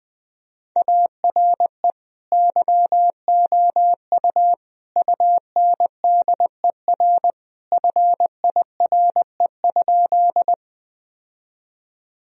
Morse code, heard as a keyed tone, about 20 wpm, 700 Hz. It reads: ARE YOU UNDER FIRE?
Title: Sentences from Top 1000 Words - Review Part 1 - 20wpm